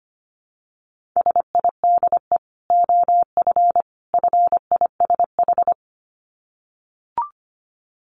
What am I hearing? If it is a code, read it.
Morse code, heard as a keyed tone, about 25 wpm, 700 Hz. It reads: SIDE OF FISH
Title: Dit Challenge 25wpm